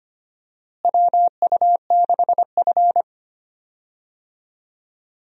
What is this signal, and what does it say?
Morse code, heard as a keyed tone, about 25 wpm, 700 Hz. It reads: WU6F
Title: USA Call Signs - 4 Characters - Repeated 3x - 25wpm